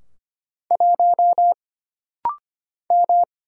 Morse code, heard as a keyed tone, about 25 wpm, 700 Hz. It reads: 1 M
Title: Single Letter-Number Alt - Rapid-Fire 25wpm